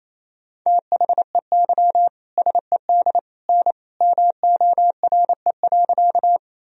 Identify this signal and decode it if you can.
Morse code, heard as a keyed tone, about 28 wpm, 700 Hz. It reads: THEY SED N MORE.
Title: Sentences from Top 100 Words and Abbreviations 28wpm